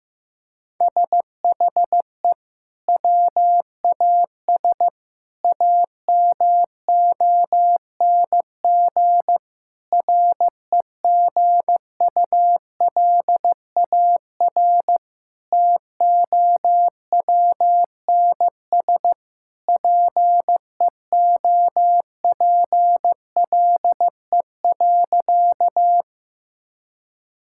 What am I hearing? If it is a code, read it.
Morse code, heard as a keyed tone, about 15 wpm, 700 Hz. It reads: SHE WAS AMONG REGULAR TOWNS PEOPLE.
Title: Sentences from Top 1000 Words - Review Part 6 - 15wpm